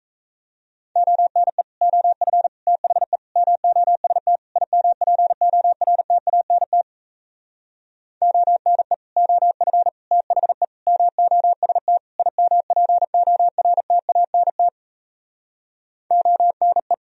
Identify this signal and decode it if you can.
Morse code, heard as a keyed tone, about 37 wpm, 700 Hz. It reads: ONE OF THE MOST IMPORTANT ONE OF THE MOST IMPORTANT ONE